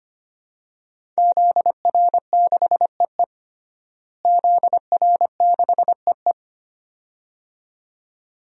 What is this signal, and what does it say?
Morse code, heard as a keyed tone, about 25 wpm, 700 Hz. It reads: ZR6EE ZR6EE
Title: DXCC-100 Call Signs - 5 Characters - Repeated 2x - 25wpm